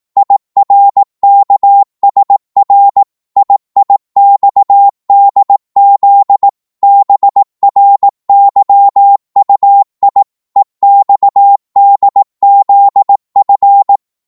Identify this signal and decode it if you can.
Morse code, heard as a keyed tone, about 18 wpm, 800 Hz. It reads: IRKSR IIXDZ BRYUI EXDZF